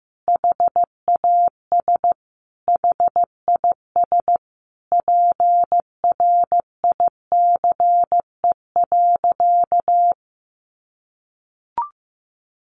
Morse code, heard as a keyed tone, about 15 wpm, 700 Hz. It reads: HAS HIS PRICE.
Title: Sentences from Top 600 Words 15wpm